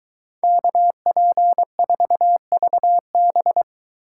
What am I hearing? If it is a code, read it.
Morse code, heard as a keyed tone, about 23 wpm, 700 Hz. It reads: KP4VB